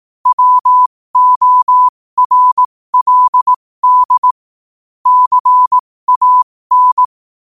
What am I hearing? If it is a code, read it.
Morse code, heard as a keyed tone, about 18 wpm, spaced as for 15 wpm, 1000 Hz. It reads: WORLD CAN